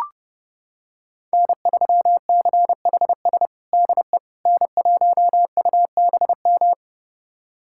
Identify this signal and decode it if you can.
Morse code, heard as a keyed tone, about 30 wpm, 700 Hz. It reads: N3CHS DE N1UBM
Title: USA Callsign DE Callsign - 5 Characters - 30wpm